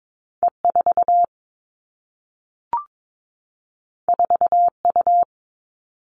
Morse code, heard as a keyed tone, about 22 wpm, 700 Hz. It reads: E 4 4 U
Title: Two Challenging Letter-Number Combinations 22wpm